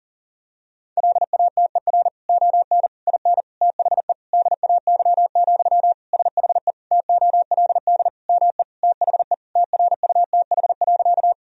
Morse code, heard as a keyed tone, about 40 wpm, 700 Hz. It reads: LATER ON IN THE DAY, SHE TOLD ME THE TRUTH.
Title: Sentences from Top 1000 Words - Review Part 4 - 40wpm